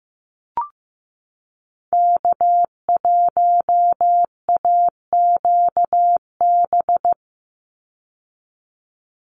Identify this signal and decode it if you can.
Morse code, heard as a keyed tone, about 15 wpm, 700 Hz. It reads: K1AQB
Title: Call Signs Repeated 3x - 15wpm